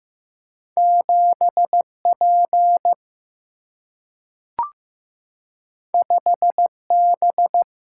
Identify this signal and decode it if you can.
Morse code, heard as a keyed tone, about 15 wpm, 700 Hz. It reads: 7 P 5 B
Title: Single-Letter Single-Number 15wpm